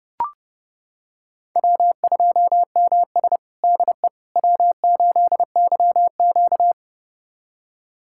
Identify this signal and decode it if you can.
Morse code, heard as a keyed tone, about 32 wpm, 700 Hz. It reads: W2MS DE W8YQ
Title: USA Callsign DE Callsign - 4 Characters - 30wpm